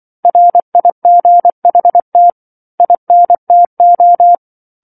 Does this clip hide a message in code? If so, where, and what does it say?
Yes: Morse code from 0.2 to 4.3 s, heard as a keyed tone, about 24 wpm, 700 Hz. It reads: RIGHT INTO